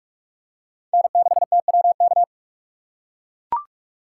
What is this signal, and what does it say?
Morse code, heard as a keyed tone, about 44 wpm, 700 Hz. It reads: N6TWK